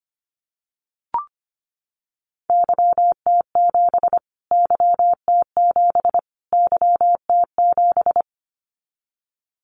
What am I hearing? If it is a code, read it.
Morse code, heard as a keyed tone, about 25 wpm, 700 Hz. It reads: YT7 YT7 YT7
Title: Callsign Prefixes for Top 25 DXCC - Repeated 3x - 25wpm